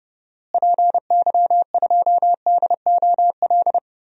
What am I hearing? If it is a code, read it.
Morse code, heard as a keyed tone, about 30 wpm, 700 Hz. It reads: PY2DOL
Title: DXCC-100 Call Signs - 6 Characters - Repeated 2x - 30wpm